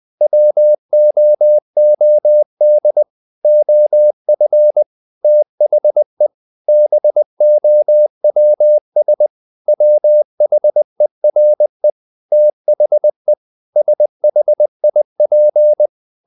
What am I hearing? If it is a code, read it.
Morse code, heard as a keyed tone, about 20 wpm, 600 Hz. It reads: WOOD OF THE BOWS WHERE THE SHIP
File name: the old man and the sea part 2 20wpm